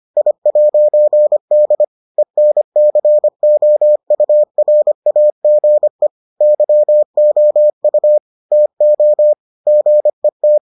Morse code, heard as a keyed tone, about 25 wpm, 600 Hz. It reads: I'D ENCOURAGE YOU TO GET